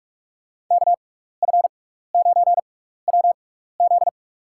Morse code, heard as a keyed tone, about 47 wpm, spaced as for 16 wpm, 700 Hz. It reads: KF9WZ
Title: USA Call Signs - 5 Characters - 6x Character Spacing - 45wpm